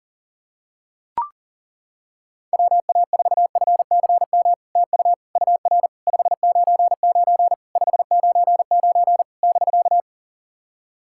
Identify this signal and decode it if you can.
Morse code, heard as a keyed tone, about 40 wpm, 700 Hz. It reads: WA4FCM TU UR 599 599 <BK>
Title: SOTA QSO Practice 40wpm